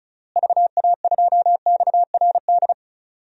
Morse code, heard as a keyed tone, about 36 wpm, 700 Hz. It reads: VA2XRD
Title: DXCC-100 Call Signs - 6 Characters - Repeated 3x - 35wpm